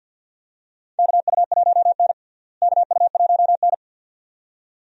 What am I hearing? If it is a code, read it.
Morse code, heard as a keyed tone, about 52 wpm, 700 Hz. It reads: KU1N KU1N
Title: USA Call Signs - 4 Characters - Repeated 2x - 50wpm